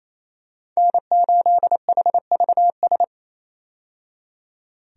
Morse code, heard as a keyed tone, about 27 wpm, 700 Hz. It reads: N8HVS
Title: USA Call Signs - 5 Characters - 28wpm